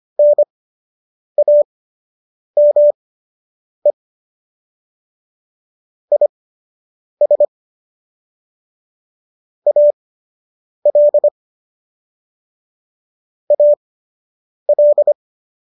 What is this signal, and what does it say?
Morse code, heard as a keyed tone, about 25 wpm, spaced as for 8 wpm, 590 Hz. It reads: NAME IS AL AL